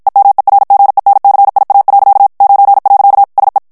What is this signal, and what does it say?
Morse code, heard as a keyed tone, about 53 wpm, 800 Hz. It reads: EMERGENCIA. 9. SE
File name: ESPAGNOL TEXTE EN CLAIR 1 VITESSE 50 MOTS MINUTES